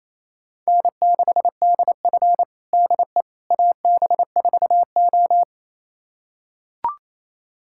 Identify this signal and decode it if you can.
Morse code, heard as a keyed tone, about 28 wpm, 700 Hz. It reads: N6DF DE AB4O
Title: USA Callsign DE Callsign - 4 Characters - 28wpm